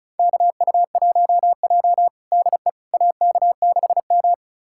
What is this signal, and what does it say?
Morse code, heard as a keyed tone, about 36 wpm, 700 Hz. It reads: KU1J DE AK6M
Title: USA Callsign DE Callsign - 4 Characters - 35wpm